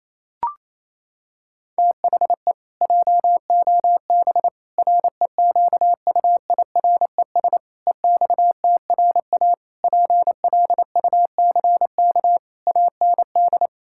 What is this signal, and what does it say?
Morse code, heard as a keyed tone, about 28 wpm, 700 Hz. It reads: THE JOB REQUIRES EXTRA PLUCK AND
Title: Pangram Challenge v2 - 28wpm